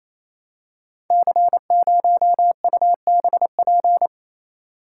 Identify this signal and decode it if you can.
Morse code, heard as a keyed tone, about 27 wpm, 700 Hz. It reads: C0UBP
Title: Code Groups - 5 Letter:Numbers - 28wpm